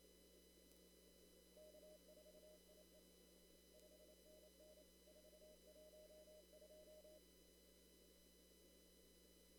Morse code, heard as a keyed tone, about 28 wpm, 610 Hz. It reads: K4IE 5NN 412
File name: CW / kt5j wpx cw-op-k5pi